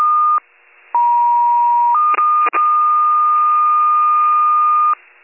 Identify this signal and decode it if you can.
Two-tone page, heard as a one-second tone, then a three-second tone, about 950 then 1220 Hz. It is heard from 0.9 to 4.9 s.